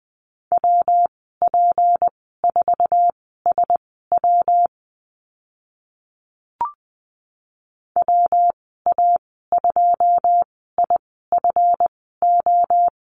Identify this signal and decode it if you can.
Morse code, heard as a keyed tone, about 19 wpm, spaced as for 14 wpm, 700 Hz. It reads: WP4SW WA2IFO
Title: 167 Lesson 27g - Number 7 - Call Signs 20wpm